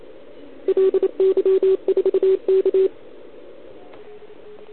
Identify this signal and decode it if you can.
Morse code, heard as a keyed tone, about 29 wpm, 410 Hz. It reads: LY4K
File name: ARCHIV / OKDX2020 / OL4Wa